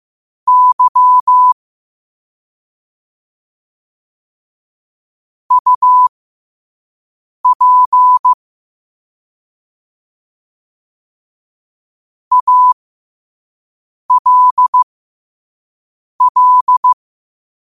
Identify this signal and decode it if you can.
Morse code, heard as a keyed tone, about 15 wpm, spaced as for 5 wpm, 1000 Hz. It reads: Y UP ALL
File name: qotdfiles / 5wpm